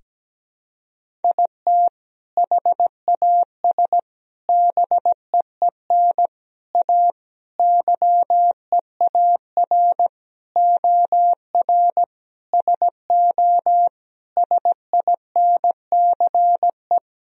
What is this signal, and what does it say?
Morse code, heard as a keyed tone, about 17 wpm, 700 Hz. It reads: IT HAS BEEN A YEAR OR SO SINCE
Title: Sentences from Top 1000 Words - Review Part 1 - 17wpm